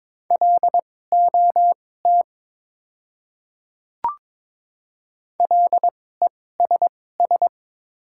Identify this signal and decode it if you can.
Morse code, heard as a keyed tone, about 22 wpm, spaced as for 16 wpm, 700 Hz. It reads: LOT LESS
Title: 2x Character Spacing - Top 500 Words - 22wpm